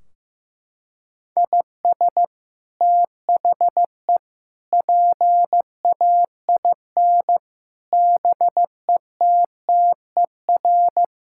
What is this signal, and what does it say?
Morse code, heard as a keyed tone, about 15 wpm, 700 Hz. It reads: IS THE PAIN BETTER
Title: Sentences from Top 1600 Words 15wpm